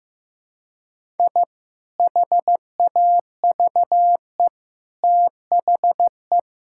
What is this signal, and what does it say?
Morse code, heard as a keyed tone, about 15 wpm, 700 Hz. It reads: I HAVE THE
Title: Sentences from Top 1000 Words - Review Part 6 - 15wpm